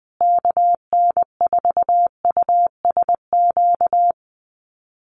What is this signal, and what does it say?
Morse code, heard as a keyed tone, about 20 wpm, 700 Hz. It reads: KN4USQ
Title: USA Call Signs Repeated 3x 20wpm